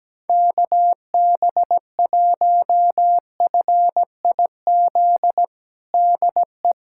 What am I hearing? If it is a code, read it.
Morse code, heard as a keyed tone, about 17 wpm, 700 Hz. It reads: KB1FIZ DE
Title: USA Callsign DE Callsign 17wpm